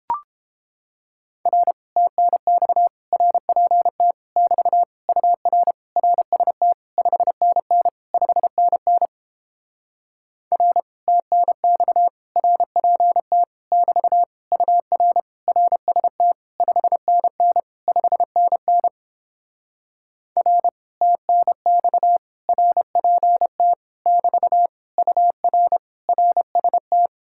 Morse code, heard as a keyed tone, about 31 wpm, 700 Hz. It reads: R TNX RPT <BT> UR RST 5NN 5NN R TNX RPT <BT> UR RST 5NN 5NN R TNX RPT <BT> UR RST